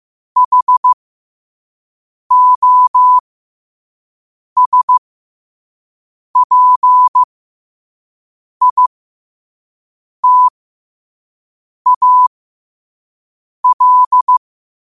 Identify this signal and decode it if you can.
Morse code, heard as a keyed tone, about 15 wpm, spaced as for 5 wpm, 1000 Hz. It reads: HOSPITAL
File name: qotdfiles / 5wpm